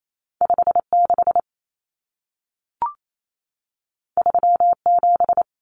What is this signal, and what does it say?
Morse code, heard as a keyed tone, about 28 wpm, 700 Hz. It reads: 56 37
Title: Numbers - 2-Digit 28wpm